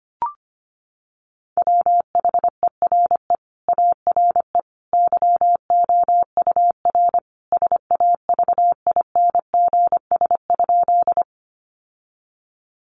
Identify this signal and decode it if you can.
Morse code, heard as a keyed tone, about 25 wpm, 700 Hz. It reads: WHERE ARE YOUR SAVINGS?
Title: Sentences from Top 1200 Words 25wpm